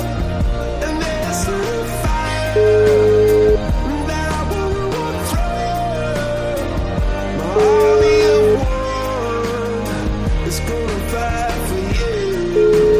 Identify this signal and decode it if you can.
Ringback tone (Europe, 425 Hz), with noise.